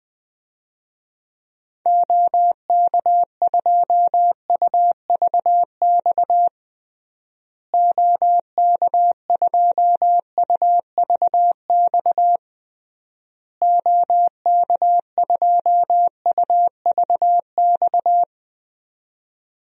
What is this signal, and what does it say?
Morse code, heard as a keyed tone, about 20 wpm, 700 Hz. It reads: OK2UVX OK2UVX OK2UVX
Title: DXCC-100 Call Signs - 6 Characters - Repeated 3x - 20wpm